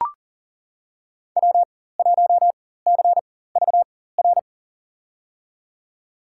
Morse code, heard as a keyed tone, about 40 wpm, spaced as for 19 wpm, 700 Hz. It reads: W1CVR
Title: USA Call Signs - 5 Characters - 4x Character Spacing - 40wpm